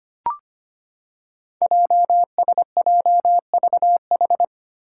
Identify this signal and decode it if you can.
Morse code, heard as a keyed tone, about 25 wpm, 700 Hz. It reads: JSJVH
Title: Five Challenging Letter-Number Combinations 25wpm